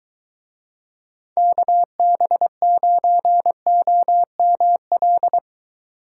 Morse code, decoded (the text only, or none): KB9OML